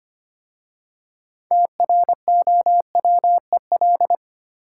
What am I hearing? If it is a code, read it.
Morse code, heard as a keyed tone, about 30 wpm, 700 Hz. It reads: TROWEL
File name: practice / 25wpm / Difficult Words - Speed-Racing with Hints - 25wpm